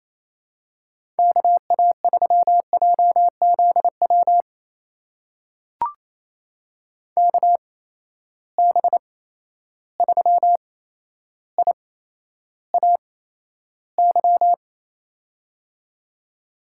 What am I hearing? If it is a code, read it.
Morse code, heard as a keyed tone, about 28 wpm, spaced as for 12 wpm, 700 Hz. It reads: KA3JZW KB3IAY
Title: USA Call Signs - 6 Characters - 8x Character Spacing - 28wpm